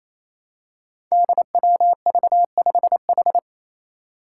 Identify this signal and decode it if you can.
Morse code, heard as a keyed tone, about 28 wpm, 700 Hz. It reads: DWV5H
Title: Five Challenging Letter-Number Combinations 28wpm